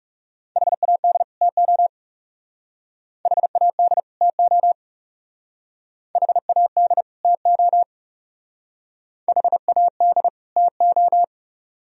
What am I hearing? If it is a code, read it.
Morse code, heard as a keyed tone, about 36 wpm, 700 Hz. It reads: HAD TO HAD TO HAD TO HAD TO
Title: Speed-Racing - Sets of 2 Words 30wpm